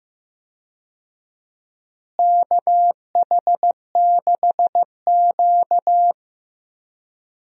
Morse code, heard as a keyed tone, about 15 wpm, 700 Hz. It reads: KH6Q
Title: DXCC-100 Call Signs - 4 Characters - Repeated 3x - 15wpm